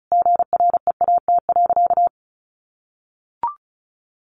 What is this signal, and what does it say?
Morse code, heard as a keyed tone, about 34 wpm, 700 Hz. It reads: GREAT.
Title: Sentences from Top 800 Words 35wpm